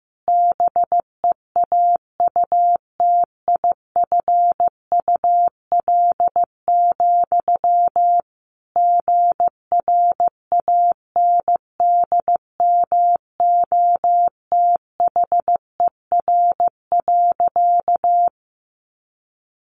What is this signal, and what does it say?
Morse code, heard as a keyed tone, about 15 wpm, 700 Hz. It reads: BEAUTIFUL, GRANDMOTHER.
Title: Sentences from Top 1200 Words 15wpm